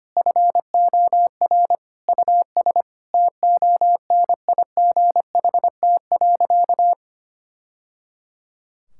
Morse code, heard as a keyed tone, about 25 wpm, 700 Hz. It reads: FOR US TONIGHT.
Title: Sentences from Top 1000 Words - Review Part 5 - 25wpm